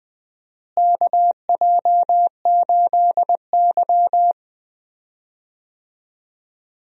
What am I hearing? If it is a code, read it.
Morse code, heard as a keyed tone, about 20 wpm, 700 Hz. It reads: KJ8Y
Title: USA Call Signs - 4 Characters - Repeated 2x - 20wpm